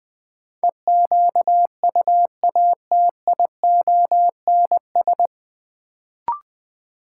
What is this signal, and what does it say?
Morse code, heard as a keyed tone, about 21 wpm, 700 Hz. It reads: EQUATIONS
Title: Sets of 5 Words - New Word Drill - 20wpm